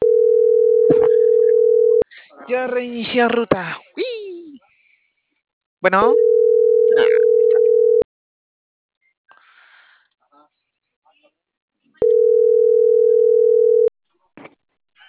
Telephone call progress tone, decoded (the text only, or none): ringback tone